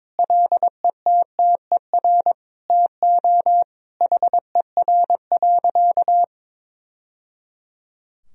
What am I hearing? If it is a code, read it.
Morse code, heard as a keyed tone, about 22 wpm, 700 Hz. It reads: LETTER TO HER.
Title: Sentences from Top 1000 Words - Review Part 3 - 22wpm